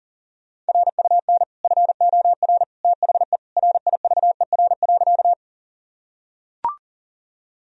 Morse code, heard as a keyed tone, about 40 wpm, 700 Hz. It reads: RUN FOR THE RIVER.
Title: Sentences from Top 1000 Words - Review Part 1 - 40wpm